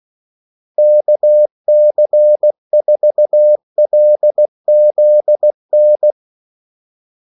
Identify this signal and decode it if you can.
Morse code, heard as a keyed tone, about 16 wpm, 600 Hz. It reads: KC4LZN
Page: Kaldesignaler 16 wpm